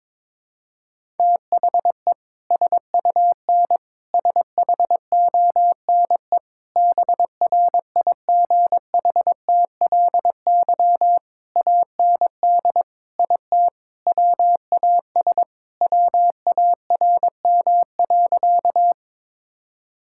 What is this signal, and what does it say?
Morse code, heard as a keyed tone, about 22 wpm, 700 Hz. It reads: THE SUN SHONE BRIGHTLY AND IT WAS WARM.
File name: practice / 22wpm / Sentences with 2-Word Phrases - 22wpm